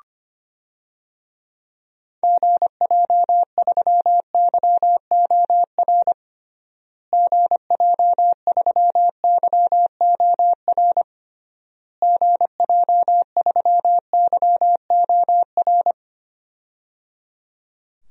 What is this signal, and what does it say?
Morse code, heard as a keyed tone, about 25 wpm, 700 Hz. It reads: GJ3YOR GJ3YOR GJ3YOR
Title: DXCC-100 Call Signs - 6 Characters - Repeated 3x - 25wpm